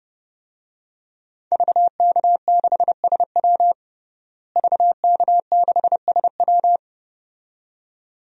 Morse code, heard as a keyed tone, about 31 wpm, 700 Hz. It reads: VK6SW VK6SW
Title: DXCC-100 Call Signs - 5 Characters - Repeated 2x - 30wpm